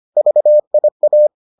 Morse code, heard as a keyed tone, about 25 wpm, 600 Hz. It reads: VIA